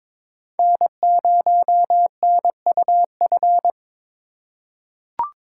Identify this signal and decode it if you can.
Morse code, heard as a keyed tone, about 22 wpm, 700 Hz. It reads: N0NUF
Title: USA Call Signs - 5 Characters - 2x Character Spacing - 22wpm